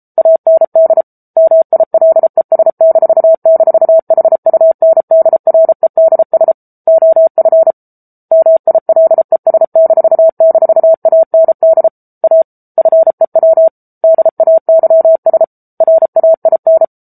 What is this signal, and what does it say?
Morse code, heard as a keyed tone, about 34 wpm, 650 Hz. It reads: AND MILES--HUNDREDS OF MILES--AND A FEW DAYS RAIN